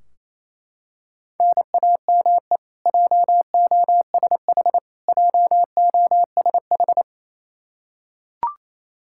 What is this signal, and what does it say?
Morse code, heard as a keyed tone, about 28 wpm, 700 Hz. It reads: NAME JOSH JOSH
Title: Practice QSO - Exchange 1 - Beginning - 28wpm